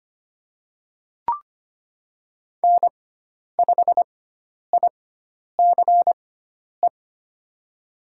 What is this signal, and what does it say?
Morse code, heard as a keyed tone, about 25 wpm, spaced as for 10 wpm, 700 Hz. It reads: N5ICE
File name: Lesson 16g - Letter c - Call Signs 25wpm